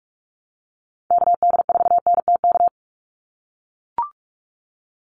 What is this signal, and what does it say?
Morse code, heard as a keyed tone, about 44 wpm, 700 Hz. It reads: KD4NTK
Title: USA Call Signs 45wpm